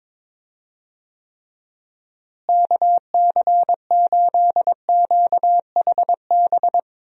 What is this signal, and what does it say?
Morse code, heard as a keyed tone, about 21 wpm, 700 Hz. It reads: KC8QHB
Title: USA Call Signs - 6 Characters - Repeated 3x - 22wpm